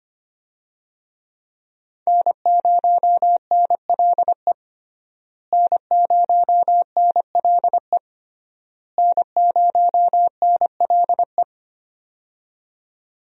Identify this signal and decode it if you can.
Morse code, heard as a keyed tone, about 25 wpm, 700 Hz. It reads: N0NLE N0NLE N0NLE
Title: USA Call Signs - 5 Characters - Repeated 3x - 25wpm